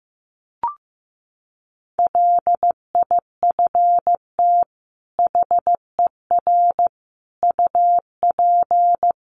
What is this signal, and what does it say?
Morse code, heard as a keyed tone, about 14 wpm, 700 Hz. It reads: LIFT HER UP
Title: Sentences from Top 1300 Words 15wpm